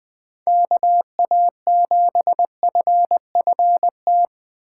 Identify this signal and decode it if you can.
Morse code, heard as a keyed tone, about 20 wpm, 700 Hz. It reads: KA7FFT
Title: USA Call Signs - 6x Character Spacing - 20wpm